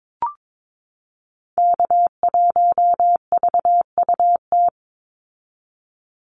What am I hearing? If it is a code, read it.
Morse code, heard as a keyed tone, about 22 wpm, 700 Hz. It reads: K1VUT
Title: Call Signs - 22wpm